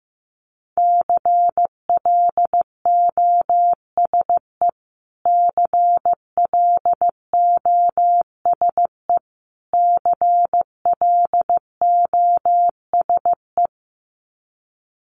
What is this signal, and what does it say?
Morse code, heard as a keyed tone, about 15 wpm, 700 Hz. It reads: CLOSE CLOSE CLOSE
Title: Top 300 Words Repeated 15wpm